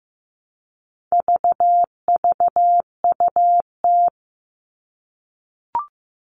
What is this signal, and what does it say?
Morse code, heard as a keyed tone, about 15 wpm, 700 Hz. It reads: VVUT
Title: All Cut Numbers - 4-Digit 15wpm